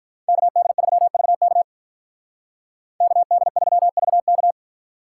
Morse code, heard as a keyed tone, about 46 wpm, 700 Hz. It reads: KD3VK KD3VK